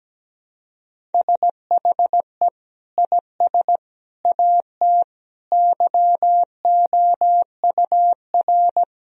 Morse code, heard as a keyed tone, about 17 wpm, 700 Hz. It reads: SHE IS AT YOUR
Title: Sentences from Top 1000 Words - Review Part 1 - 17wpm